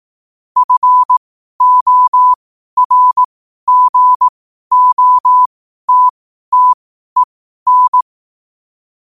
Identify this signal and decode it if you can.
Morse code, heard as a keyed tone, about 18 wpm, spaced as for 12 wpm, 1000 Hz. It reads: FORGOTTEN
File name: qotdfiles / 12wpm